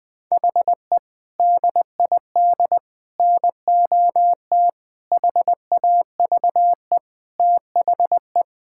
Morse code, heard as a keyed tone, about 20 wpm, 700 Hz. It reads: HE DID NOT HAVE THE
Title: Sentences with 4-Word Phrases - 20wpm